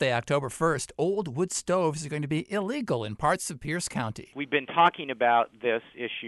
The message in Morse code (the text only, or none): none